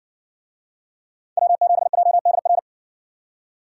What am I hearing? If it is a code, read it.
Morse code, heard as a keyed tone, about 56 wpm, 700 Hz. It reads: W7JDR